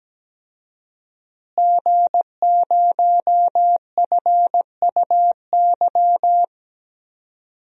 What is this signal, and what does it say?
Morse code, heard as a keyed tone, about 17 wpm, 700 Hz. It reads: G0FUY
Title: DXCC-100 Call Signs - 5 Characters - Repeated 2x - 17wpm